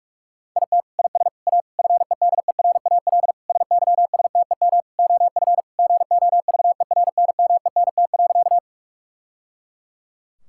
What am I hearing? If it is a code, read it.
Morse code, heard as a keyed tone, about 44 wpm, 700 Hz. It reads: IT IS A FEDERAL SYSTEM OF GOVERNMENT.
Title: Sentences from Top 1000 Words - Review Part 5 - 45wpm